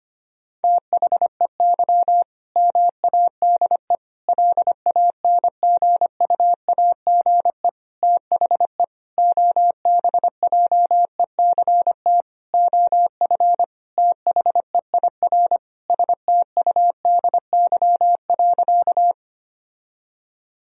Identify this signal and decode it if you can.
Morse code, heard as a keyed tone, about 25 wpm, 700 Hz. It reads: THEY MADE LANGUAGE THE OBJECT OF THEIR STUDY.
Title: Sentences from Top 1000 Words - Review Part 3 - 25wpm